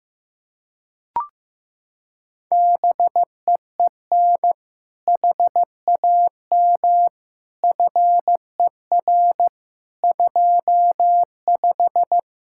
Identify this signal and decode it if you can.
Morse code, heard as a keyed tone, about 15 wpm, 700 Hz. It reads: BEEN HAM FER 25
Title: K1USN SST QSO Practice 15wpm